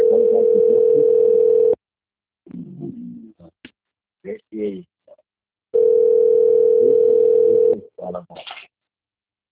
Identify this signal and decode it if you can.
Ringback tone (North America), 440 + 480 Hz, with speech.
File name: rg-1600-unknown- - - .26529